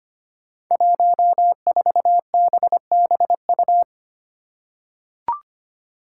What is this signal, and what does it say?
Morse code, heard as a keyed tone, about 25 wpm, 700 Hz. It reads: 14BBU
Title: Five Challenging Letter-Number Combinations 25wpm